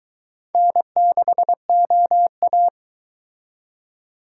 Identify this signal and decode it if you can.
Morse code, heard as a keyed tone, about 23 wpm, 700 Hz. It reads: N6OA